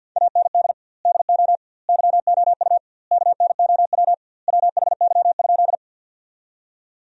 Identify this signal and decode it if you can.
Morse code, heard as a keyed tone, about 51 wpm, 700 Hz. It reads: AND DO YOU KNOW WHY?
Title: 2x Word-Spacing - Sentences from Top 200 Words - 50wpm